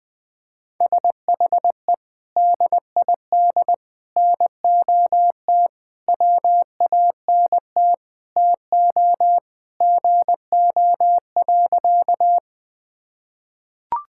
Morse code, heard as a keyed tone, about 20 wpm, 700 Hz. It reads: SHE DID NOT WANT TO GO.
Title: Sentences with 4-Word Phrases - 20wpm